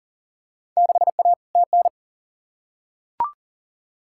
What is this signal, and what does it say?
Morse code, heard as a keyed tone, about 40 wpm, 700 Hz. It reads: 6A TN
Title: Field Day Exchanges Encore - 40wpm